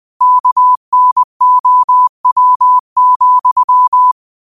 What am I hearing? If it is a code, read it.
Morse code, heard as a keyed tone, about 20 wpm, 1000 Hz. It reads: KNOW,
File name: 20wpm